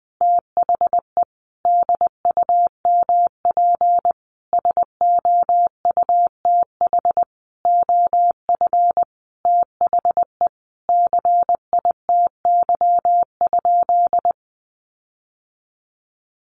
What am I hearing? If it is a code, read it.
Morse code, heard as a keyed tone, about 20 wpm, 700 Hz. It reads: THE DUMP SOUTH OF THE CITY?
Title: Sentences from Top 1300 Words 20wpm